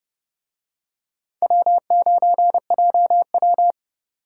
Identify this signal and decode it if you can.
Morse code, heard as a keyed tone, about 29 wpm, 700 Hz. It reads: W9JW